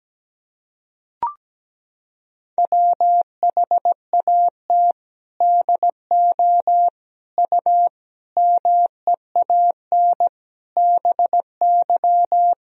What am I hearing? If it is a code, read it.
Morse code, heard as a keyed tone, about 17 wpm, 700 Hz. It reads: WHAT DO U MEAN BY
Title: Sentences from Top 200 Words and Abbreviations 17wpm